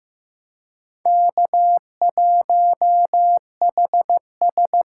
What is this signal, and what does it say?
Morse code, heard as a keyed tone, about 18 wpm, 700 Hz. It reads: K1HS
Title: Speed Racing - USA 1x2 Call Signs 15wpm